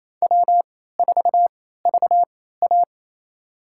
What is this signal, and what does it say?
Morse code, heard as a keyed tone, about 28 wpm, spaced as for 16 wpm, 700 Hz. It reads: W4VA
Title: USA Call Signs - 4 Characters - 3x Character Spacing - 28wpm